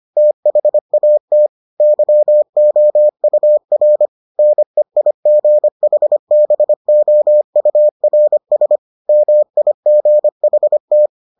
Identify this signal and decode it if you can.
Morse code, heard as a keyed tone, about 25 wpm, 600 Hz. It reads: THAT YOUR NEIGHBOURS MIGHT